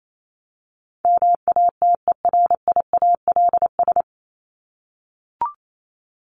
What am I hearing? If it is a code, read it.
Morse code, heard as a keyed tone, about 28 wpm, 700 Hz. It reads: MATERIALS
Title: ES:S Suffix - Top 100 Words with Base Words 28wpm